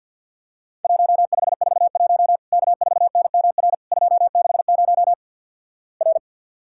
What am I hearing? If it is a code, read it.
Morse code, heard as a keyed tone, about 50 wpm, 700 Hz. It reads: 1541 K4NMR 260 R
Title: Worked All Europe DX Contest - QTC Exchange with Repeats - 50wpm